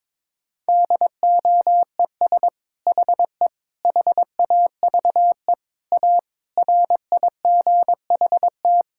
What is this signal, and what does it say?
Morse code, heard as a keyed tone, about 22 wpm, 700 Hz. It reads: DOES HE HAVE A RIGHT
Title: Sentences from Top 1000 Words - Review Part 5 - 22wpm